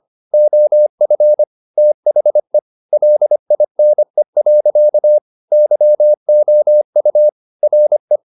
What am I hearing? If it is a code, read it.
Morse code, heard as a keyed tone, about 25 wpm, 600 Hz. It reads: OF THE LINE. YOU RE
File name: the old man and the sea part 1 25wpm